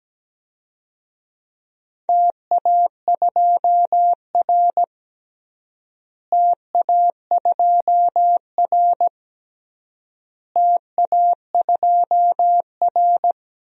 Morse code, heard as a keyed tone, about 17 wpm, 700 Hz. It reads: TA2R TA2R TA2R
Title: DXCC-100 Call Signs - 4 Characters - Repeated 3x - 17wpm